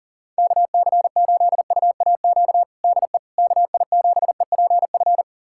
Morse code, heard as a keyed tone, about 40 wpm, 700 Hz. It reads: KC8UAQ DE KI7EPF
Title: USA Callsign DE Callsign - 6 Characters - 40wpm